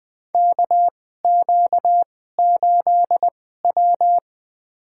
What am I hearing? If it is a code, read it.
Morse code, heard as a keyed tone, about 20 wpm, spaced as for 14 wpm, 700 Hz. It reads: KQ8W
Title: USA Call Signs - 4 Characters - 2x Character Spacing - 20wpm